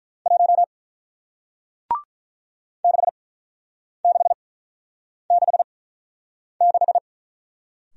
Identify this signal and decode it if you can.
Morse code, heard as a keyed tone, about 48 wpm, spaced as for 9 wpm, 700 Hz. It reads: 1 6666